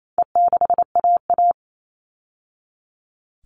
Morse code, heard as a keyed tone, about 28 wpm, 700 Hz. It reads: E6AA
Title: All Cut Numbers - 4-Digit 28wpm